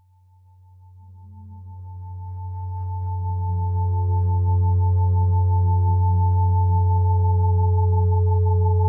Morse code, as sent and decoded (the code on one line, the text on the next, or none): none